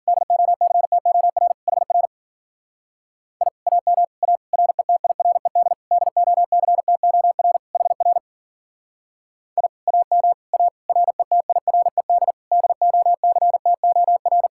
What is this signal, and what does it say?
Morse code, heard as a keyed tone, about 50 wpm, 700 Hz. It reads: DOCTOR HR I AM A RETIRED DOCTOR HR I AM A RETIRED DOCTOR
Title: Speed-Racing Common QSO Phrases - 40wpm